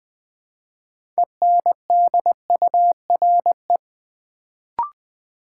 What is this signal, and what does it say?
Morse code, heard as a keyed tone, about 20 wpm, 700 Hz. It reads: ENDURE
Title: Build A Word - 2 Letters At A Time - 6-Letter Words - 20wpm